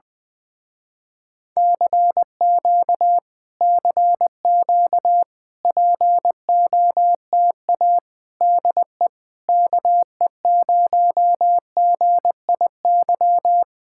Morse code, heard as a keyed tone, about 20 wpm, 700 Hz. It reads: CQ CQ POTA DE KE0GIY
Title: Speed-Racing Common QSO Phrases Encore - 17wpm